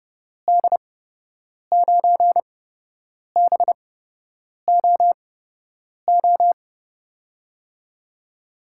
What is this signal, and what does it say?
Morse code, heard as a keyed tone, about 30 wpm, spaced as for 8 wpm, 700 Hz. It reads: D9BOO
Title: USA Call Signs - 6 Characters - 8x Character Spacing - 30wpm